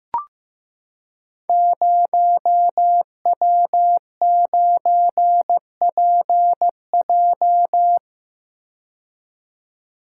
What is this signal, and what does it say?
Morse code, heard as a keyed tone, about 15 wpm, 700 Hz. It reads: 0W9PJ
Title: Code Groups - 5 Letter:Numbers - 15wpm